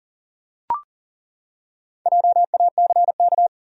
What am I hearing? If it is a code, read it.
Morse code, heard as a keyed tone, about 40 wpm, 700 Hz. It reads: JACK
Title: USA Common Names - 40wpm